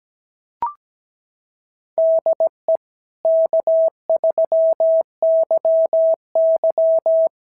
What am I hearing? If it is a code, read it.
Morse code, heard as a keyed tone, about 17 wpm, 690 Hz. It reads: DE K3YY
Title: SOTA QSO Practice 17wpm